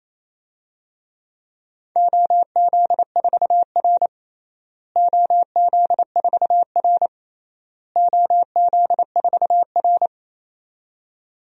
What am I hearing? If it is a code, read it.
Morse code, heard as a keyed tone, about 28 wpm, 700 Hz. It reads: OZ4R OZ4R OZ4R
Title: DXCC-100 Call Signs - 4 Characters - Repeated 3x - 28wpm